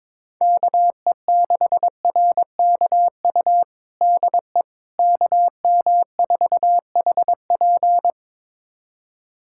Morse code, heard as a keyed tone, about 22 wpm, 700 Hz. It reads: KE6RKU DE KM4HP
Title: USA Callsign DE Callsign 22wpm